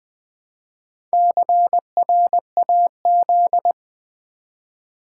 Morse code, heard as a keyed tone, about 21 wpm, 700 Hz. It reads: CRAZ
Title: Build A Word - 1 Letter At A Time - 3 to 5-Letter Words - 20wpm